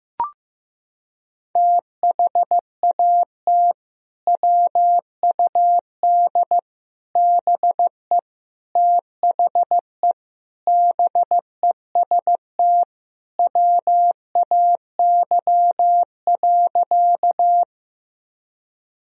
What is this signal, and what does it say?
Morse code, heard as a keyed tone, about 16 wpm, 700 Hz. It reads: THAT WUD BE THE BEST WAY.
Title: Sentences from Top 400 Words and Abbreviations 15wpm